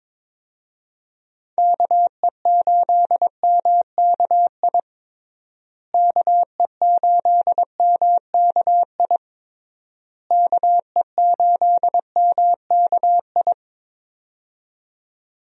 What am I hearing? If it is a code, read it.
Morse code, heard as a keyed tone, about 22 wpm, 700 Hz. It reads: KE8MKI KE8MKI KE8MKI
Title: USA Call Signs - 6 Characters - Repeated 3x - 22wpm